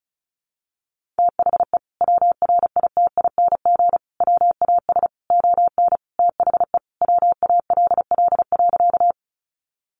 Morse code, heard as a keyed tone, about 34 wpm, 700 Hz. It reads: THE WRITING WAS ON THE WALL.
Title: Sentences from Top 1100 Words 35wpm